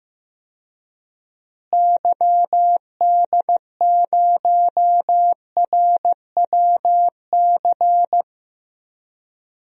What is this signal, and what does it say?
Morse code, heard as a keyed tone, about 15 wpm, 700 Hz. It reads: YD0RWC
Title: DXCC-100 Call Signs - 6 Characters - Repeated 2x - 15wpm